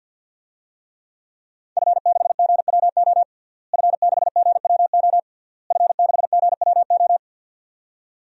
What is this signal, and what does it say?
Morse code, heard as a keyed tone, about 52 wpm, 700 Hz. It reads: F6GWO F6GWO F6GWO
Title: DXCC-100 Call Signs - 5 Characters - Repeated 3x - 50wpm